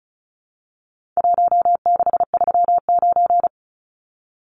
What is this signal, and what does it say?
Morse code, heard as a keyed tone, about 34 wpm, 700 Hz. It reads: 1639
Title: Numbers - Contest Style - 4-Digit 35wpm